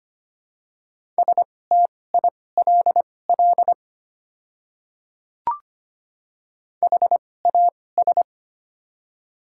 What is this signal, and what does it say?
Morse code, heard as a keyed tone, about 25 wpm, spaced as for 18 wpm, 690 Hz. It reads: STILL HAS
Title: 2x Character Spacing - Top 300 Words - 25wpm